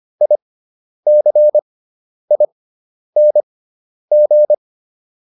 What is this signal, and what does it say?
Morse code, heard as a keyed tone, about 25 wpm, spaced as for 10 wpm, 600 Hz. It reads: ICING